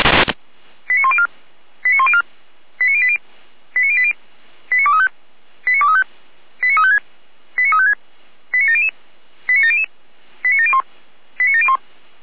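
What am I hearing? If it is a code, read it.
Five-tone selective call (ZVEI), heard as quick runs of five tones, tones of about 70 ms. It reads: 89104 89104 89080 89080 89235 89235 89356 89356 89700 89700 89701 89701
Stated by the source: ZVEI five-tone selective call received via SDR.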